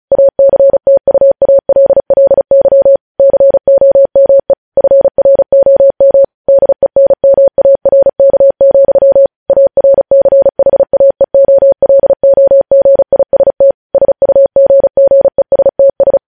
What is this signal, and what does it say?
Morse code, heard as a keyed tone, about 36 wpm, 550 Hz. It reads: ACTUALLY COME FROM DENMARK, ARCHAEOLOGIST SUGGESTS
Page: News headlines in Morse code at 35 wpm